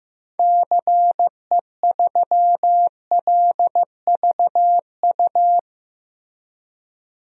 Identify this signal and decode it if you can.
Morse code, heard as a keyed tone, about 15 wpm, 700 Hz. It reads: CE3LVU
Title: DXCC-100 Call Signs - 6 Characters - Repeated 2x - 15wpm